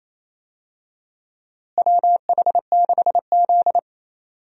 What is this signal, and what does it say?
Morse code, heard as a keyed tone, about 28 wpm, 700 Hz. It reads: WH6Z
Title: USA Call Signs - 4 Characters - Repeated 2x - 28wpm